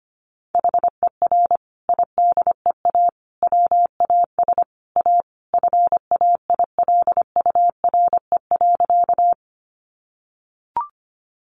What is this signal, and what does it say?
Morse code, heard as a keyed tone, about 25 wpm, 700 Hz. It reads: HER IDEA WAS A FAILURE.
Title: Sentences from Top 900 Words 25wpm